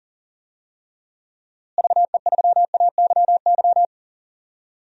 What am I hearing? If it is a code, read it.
Morse code, heard as a keyed tone, about 40 wpm, 700 Hz. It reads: VE3AYY
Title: DXCC-100 Call Signs - 6 Characters - Repeated 3x - 40wpm